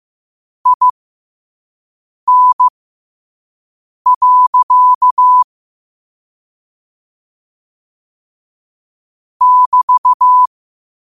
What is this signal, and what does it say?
Morse code, heard as a keyed tone, about 15 wpm, spaced as for 5 wpm, 1000 Hz. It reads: IN. <BT>
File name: 5wpm